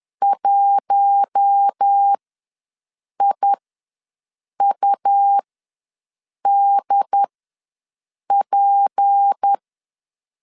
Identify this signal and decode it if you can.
Morse code, heard as a keyed tone, about 10 wpm, spaced as for 6 wpm, 800 Hz. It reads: 1IUDP
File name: LEZIONI / 6 PKX0 / 6 P ws 10wpm 3751